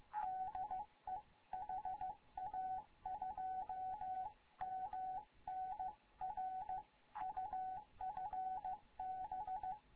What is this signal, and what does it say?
Morse code, heard as a keyed tone, about 15 wpm, 750 Hz. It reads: DE HA2MN R UFB
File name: Morze-HA2MN / hangp / 2 11 GYAKORLAT